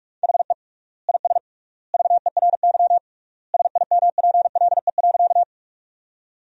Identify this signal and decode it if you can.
Morse code, heard as a keyed tone, about 46 wpm, 700 Hz. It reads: HE IS VERY SIMPLE.
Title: 3x Word-Spacing - Sentences from Top 400 Words - 45wpm